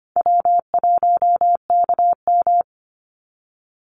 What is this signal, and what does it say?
Morse code, heard as a keyed tone, about 25 wpm, 700 Hz. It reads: W1KM
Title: Easy Call Signs - 25wpm